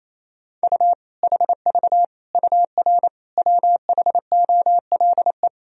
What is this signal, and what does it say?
Morse code, heard as a keyed tone, about 27 wpm, 700 Hz. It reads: U HV UR WHOLE
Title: Sentences from Top 400 Words and Abbreviations 28wpm